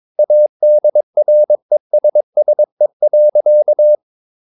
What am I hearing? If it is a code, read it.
Morse code, heard as a keyed tone, about 22 wpm, 600 Hz. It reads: ADRESSE.
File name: Ferie QRS 22wpm